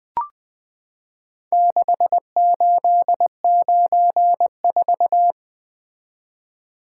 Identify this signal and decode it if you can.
Morse code, heard as a keyed tone, about 20 wpm, 700 Hz. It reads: 6894
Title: Four Letter-Number Combinations - 20wpm